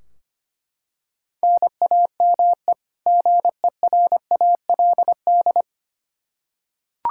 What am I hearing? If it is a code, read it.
Morse code, heard as a keyed tone, about 25 wpm, 700 Hz. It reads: NAME GERALD
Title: Practice QSO - Exchange 1 - Beginning - 25wpm